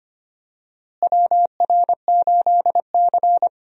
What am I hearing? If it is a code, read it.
Morse code, heard as a keyed tone, about 25 wpm, 700 Hz. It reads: WR8C
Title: USA Call Signs - 4 Characters - Repeated 2x - 25wpm